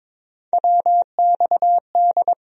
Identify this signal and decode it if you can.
Morse code, heard as a keyed tone, about 21 wpm, 700 Hz. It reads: WXD
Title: USA Call Sign Postfixes - 3 Letters 22wpm